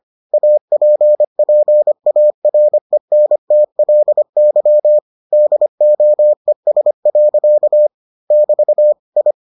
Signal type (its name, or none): Morse code